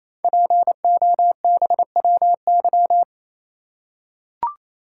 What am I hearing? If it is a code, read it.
Morse code, heard as a keyed tone, about 28 wpm, 700 Hz. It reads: POBWY
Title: Code Groups - 5 Letters - 28wpm